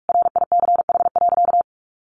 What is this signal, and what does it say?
Morse code, heard as a keyed tone, about 44 wpm, 700 Hz. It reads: RICH.
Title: Sentences from Top 1100 Words 45wpm